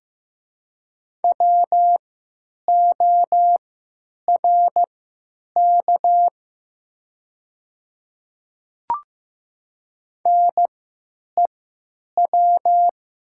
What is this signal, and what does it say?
Morse code, heard as a keyed tone, about 15 wpm, spaced as for 9 wpm, 700 Hz. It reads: WORK NEW
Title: 3x Character Spacing - Top 200 Words - 15wpm